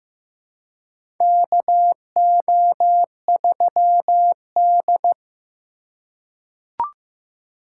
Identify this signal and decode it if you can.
Morse code, heard as a keyed tone, about 15 wpm, 700 Hz. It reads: KO3D
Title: USA Call Signs - 3x Character Spacing - 15wpm